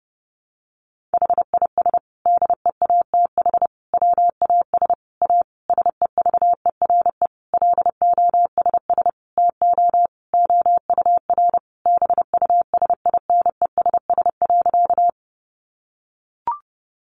Morse code, heard as a keyed tone, about 30 wpm, 700 Hz. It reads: HIS DEATH WAS A SEVERE LOSS TO OUR BUSINESS.
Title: Sentences from Top 1400 Words 30wpm